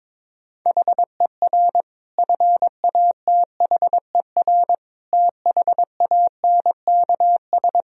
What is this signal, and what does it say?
Morse code, heard as a keyed tone, about 22 wpm, 700 Hz. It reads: HER FATHER THANKS
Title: Sentences from Top 1000 Words - Review Part 3 - 22wpm